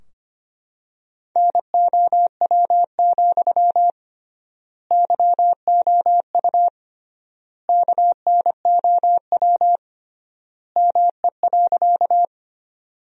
Morse code, heard as a keyed tone, about 25 wpm, 700 Hz. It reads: NOW, YOU KNOW ME.
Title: 3x Word-Spacing - Sentences from Top 200 Words - 25wpm